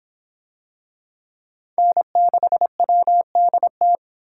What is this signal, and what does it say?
Morse code, heard as a keyed tone, about 25 wpm, 700 Hz. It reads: N6WDT